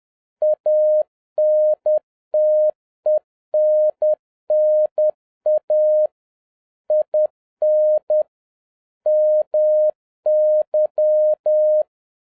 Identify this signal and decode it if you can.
Morse code, heard as a keyed tone, about 10 wpm, 610 Hz. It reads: ANTENNA IN MY
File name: 2011-07-mp3 / n41 EXCITING QRP CW OPERATION 10wpm